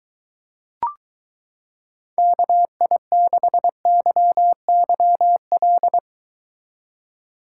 Morse code, heard as a keyed tone, about 23 wpm, 700 Hz. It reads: KI6YYL